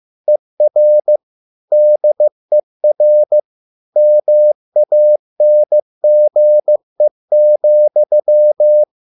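Morse code, heard as a keyed tone, about 15 wpm, 600 Hz. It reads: ER DER MANGE,
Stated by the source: Morse code at a slow 15 wpm from CW med Gnister.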